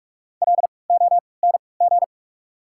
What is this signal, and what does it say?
Morse code, heard as a keyed tone, about 46 wpm, spaced as for 25 wpm, 690 Hz. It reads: LONG
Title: 3x Character Spacing - Top 100 Words - 45wpm